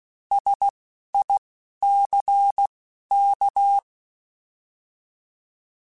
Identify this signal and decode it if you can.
Morse code, heard as a keyed tone, about 16 wpm, spaced as for 11 wpm, 800 Hz. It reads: SICK
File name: Alphabetical / Alphabetical mp3s / Ss @10